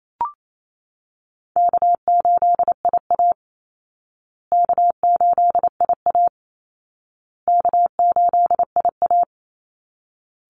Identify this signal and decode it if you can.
Morse code, heard as a keyed tone, about 28 wpm, 700 Hz. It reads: K8IA K8IA K8IA
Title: Easy Call Signs Repeated 3x - 28wpm